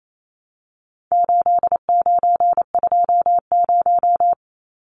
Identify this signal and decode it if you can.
Morse code, heard as a keyed tone, about 28 wpm, 700 Hz. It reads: 8920
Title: Numbers - 4-Digit 28wpm